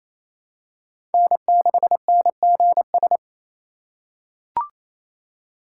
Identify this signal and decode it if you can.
Morse code, heard as a keyed tone, about 28 wpm, 700 Hz. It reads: N6NGS
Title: USA Call Signs - 5 Characters - 2x Character Spacing - 28wpm